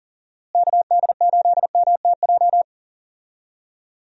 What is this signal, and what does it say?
Morse code, heard as a keyed tone, about 40 wpm, 700 Hz. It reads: KD8MTJ
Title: USA Call Signs - 6 Characters - Repeated 3x - 40wpm